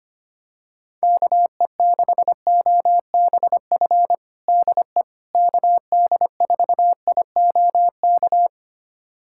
Morse code, heard as a keyed tone, about 25 wpm, 700 Hz. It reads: KE6OBF DE KD4IOK
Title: USA Callsign DE Callsign 25wpm